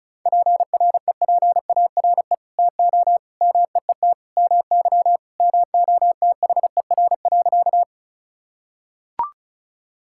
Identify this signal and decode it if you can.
Morse code, heard as a keyed tone, about 36 wpm, 700 Hz. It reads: PREPARE TO MEET MY MOTHER.
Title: Sentences from Top 1000 Words - Review Part 6 - 35wpm